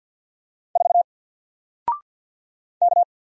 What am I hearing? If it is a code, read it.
Morse code, heard as a keyed tone, about 50 wpm, 700 Hz. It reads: 4 K
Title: Single Letter-Number 50wpm